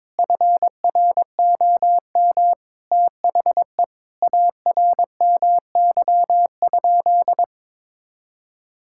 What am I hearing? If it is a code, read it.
Morse code, heard as a keyed tone, about 21 wpm, 700 Hz. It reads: FROM THE ARMY?
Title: Sentences from Top 1000 Words - Review Part 5 - 22wpm